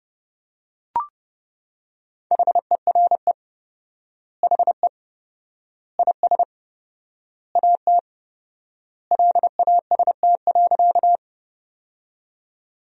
Morse code, heard as a keyed tone, about 32 wpm, 700 Hz. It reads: HERE HE IS AT LAST.
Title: 4x Word-Spacing - Sentences from Top 300 Words - 30wpm